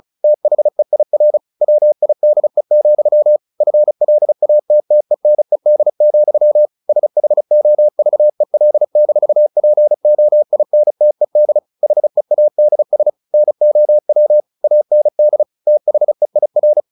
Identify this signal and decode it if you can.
Morse code, heard as a keyed tone, about 34 wpm, 600 Hz. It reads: THEIR WIDE, FLATTENED, SHOVEL-POINTED HEADS NOW AND THEIR
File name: the old man and the sea 2 35wpm